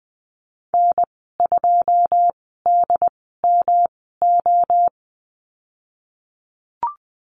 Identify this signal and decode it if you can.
Morse code, heard as a keyed tone, about 20 wpm, spaced as for 15 wpm, 700 Hz. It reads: N2DMO
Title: 128 Lesson 21g - Number 6 - Call Signs 20wpm